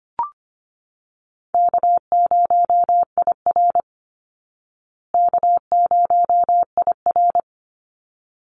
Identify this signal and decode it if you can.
Morse code, heard as a keyed tone, about 25 wpm, 700 Hz. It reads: K0IR K0IR
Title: Easy Call Signs Repeated 2x - 25wpm